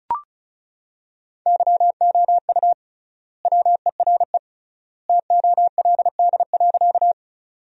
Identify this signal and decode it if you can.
Morse code, heard as a keyed tone, about 36 wpm, 700 Hz. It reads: YOU WERE TOLD.
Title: 3x Word-Spacing - Sentences from Top 400 Words - 35wpm